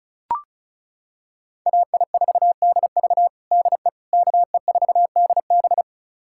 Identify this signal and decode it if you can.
Morse code, heard as a keyed tone, about 36 wpm, 700 Hz. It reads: AI4DV DE KE4DB
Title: USA Callsign DE Callsign - 5 Characters - 35wpm